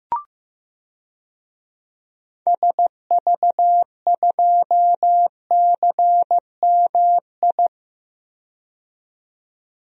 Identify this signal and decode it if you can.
Morse code, heard as a keyed tone, about 15 wpm, 700 Hz. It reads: SV2CMI
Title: DXCC-100 Call Signs - 6 Characters - 15wpm